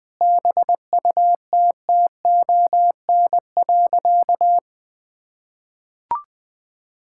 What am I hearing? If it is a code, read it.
Morse code, heard as a keyed tone, about 20 wpm, 700 Hz. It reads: BUTTON.
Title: Sentences with 6-Word Phrases - 20wpm